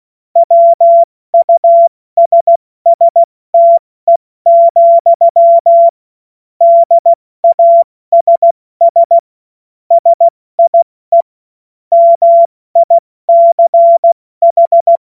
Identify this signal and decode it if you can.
Morse code, heard as a keyed tone, about 16 wpm, 680 Hz. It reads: WUSSTE, DASS SIE MICH
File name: Morsekurs / Witze / Witz 15. November 2022 70 F80